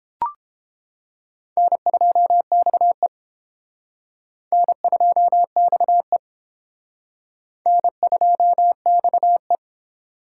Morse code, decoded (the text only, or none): N2XE N2XE N2XE